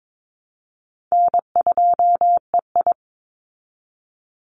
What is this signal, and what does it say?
Morse code, heard as a keyed tone, about 22 wpm, 700 Hz. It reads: N2EI
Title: Easy Call Signs Repeated 2x - 22wpm